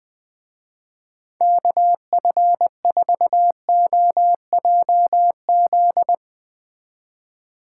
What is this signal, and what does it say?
Morse code, heard as a keyed tone, about 20 wpm, 700 Hz. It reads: KF4OJZ